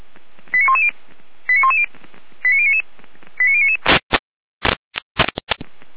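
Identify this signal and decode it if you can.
Five-tone selective call (ZVEI), heard as quick runs of five tones, tones of about 70 ms. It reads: 89110 89110 89099 89099
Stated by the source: Received on an SDR; ZVEI five-tone call 89072.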